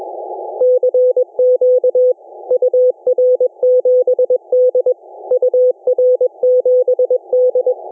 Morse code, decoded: CQ UR7D UR7D